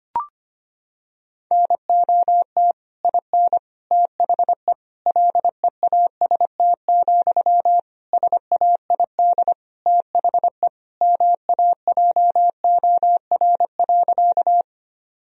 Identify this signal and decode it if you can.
Morse code, heard as a keyed tone, about 25 wpm, 700 Hz. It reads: NOT IN THE LEAST, SAID THE MAJOR.
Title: Sentences from Top 1000 Words - Review Part 4 - 25wpm